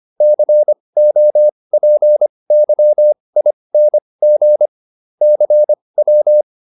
Morse code, heard as a keyed tone, about 25 wpm, spaced as for 20 wpm, 600 Hz. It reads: COPYING CW